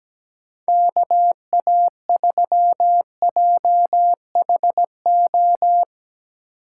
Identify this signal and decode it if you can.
Morse code, heard as a keyed tone, about 17 wpm, 700 Hz. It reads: KA3JHO
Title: USA Call Signs - 6 Characters - 4x Character Spacing - 17wpm